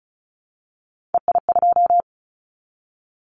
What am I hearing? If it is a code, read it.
Morse code, heard as a keyed tone, about 36 wpm, 690 Hz. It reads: EI2
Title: Callsign Prefixes for Top 50 DXCC - 35wpm